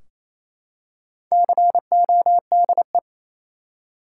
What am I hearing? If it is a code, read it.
Morse code, heard as a keyed tone, about 28 wpm, 700 Hz. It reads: CODE
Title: Four Letter Words - 28wpm